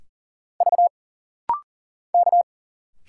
Morse code, heard as a keyed tone, about 40 wpm, 700 Hz. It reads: V K
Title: Single Letters - Rapid-Fire 40wpm